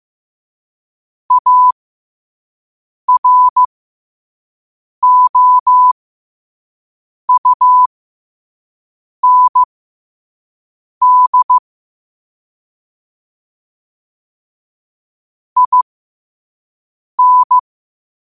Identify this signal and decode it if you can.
Morse code, heard as a keyed tone, about 15 wpm, spaced as for 5 wpm, 1000 Hz. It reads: AROUND IN
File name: qotdfiles / 5wpm